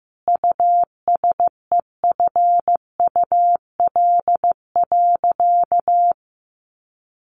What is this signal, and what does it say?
Morse code, heard as a keyed tone, about 15 wpm, 700 Hz. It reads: USEFUL.
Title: Sentences from Top 1500 Words 15wpm